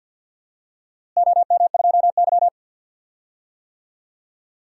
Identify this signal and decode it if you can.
Morse code, heard as a keyed tone, about 51 wpm, 700 Hz. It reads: OM2Y
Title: DXCC-100 Call Signs - 4 Characters - Repeated 3x - 50wpm